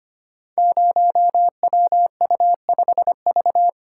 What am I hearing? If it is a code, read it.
Morse code, heard as a keyed tone, about 25 wpm, 700 Hz. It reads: 0WU5V
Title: Code Groups - 5 Letter:Numbers - 25wpm